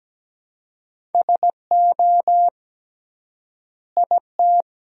Morse code, heard as a keyed tone, about 17 wpm, 700 Hz. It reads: SO IT
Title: 3x Word-Spacing - Sentences from Top 100 Words - 17wpm